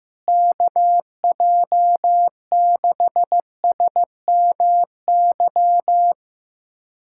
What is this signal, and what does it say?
Morse code, heard as a keyed tone, about 15 wpm, 700 Hz. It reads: KJ6SMY
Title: USA Callsign DE Callsign - 6 Characters - 15wpm